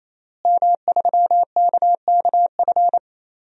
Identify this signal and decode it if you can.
Morse code, heard as a keyed tone, about 27 wpm, 700 Hz. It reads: M3KKF
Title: DXCC-100 Call Signs - 5 Characters - Repeated 3x - 28wpm